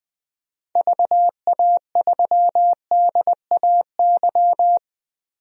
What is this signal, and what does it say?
Morse code, heard as a keyed tone, about 20 wpm, 700 Hz. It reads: VA3DAY
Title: DXCC-100 Call Signs - 6 Characters - Repeated 3x - 20wpm